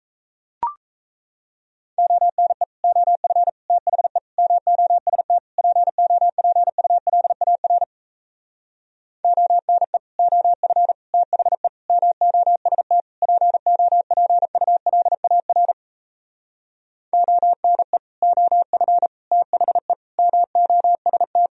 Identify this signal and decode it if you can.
Morse code, heard as a keyed tone, about 38 wpm, 700 Hz. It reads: ONE OF THE MOST POPULAR ONE OF THE MOST POPULAR ONE OF THE MOST